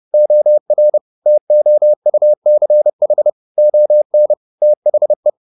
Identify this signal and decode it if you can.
Morse code, heard as a keyed tone, about 30 wpm, 600 Hz. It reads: OR TOUCH ON THE
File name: the old man and the sea / the old man and the sea 1 30wpm